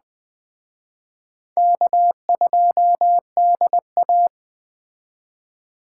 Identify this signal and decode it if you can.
Morse code, heard as a keyed tone, about 20 wpm, 700 Hz. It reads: K2DA